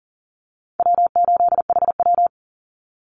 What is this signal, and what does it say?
Morse code, heard as a keyed tone, about 38 wpm, 700 Hz. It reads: W8HW
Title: Call Signs - 40wpm